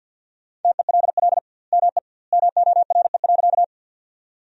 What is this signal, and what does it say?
Morse code, heard as a keyed tone, about 48 wpm, 700 Hz. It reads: TELL ME MORE.
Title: 2x Word-Spacing - Sentences from Top 300 Words - 50wpm